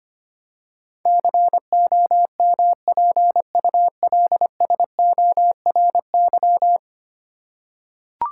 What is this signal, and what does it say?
Morse code, heard as a keyed tone, about 25 wpm, 700 Hz. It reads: COMPULSORY
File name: 10 Letter Words - First 3 Letters Given - 25wpm